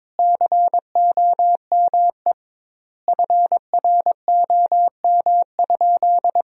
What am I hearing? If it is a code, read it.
Morse code, heard as a keyed tone, about 22 wpm, 700 Hz. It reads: COME FROM?
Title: 2x Word-Spacing - Sentences from Top 200 Words - 22wpm